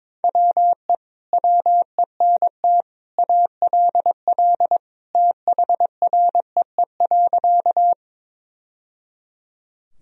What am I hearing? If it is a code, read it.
Morse code, heard as a keyed tone, about 22 wpm, 700 Hz. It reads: WE WENT ALL THREE.
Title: Sentences from Top 1000 Words - Review Part 1 - 22wpm